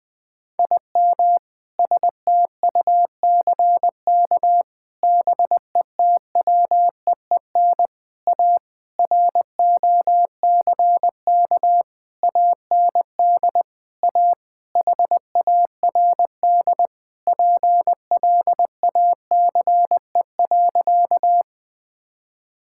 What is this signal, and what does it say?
Morse code, heard as a keyed tone, about 21 wpm, 700 Hz. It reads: IM STUCK BETWEEN A ROCK AND A HARD PLACE.
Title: Sentences with 7-Word Phrases - 20wpm